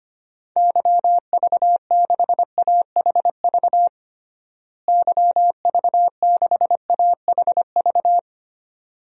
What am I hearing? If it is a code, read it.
Morse code, heard as a keyed tone, about 25 wpm, 700 Hz. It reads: YV6AHV YV6AHV
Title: DXCC-100 Call Signs - 6 Characters - Repeated 2x - 25wpm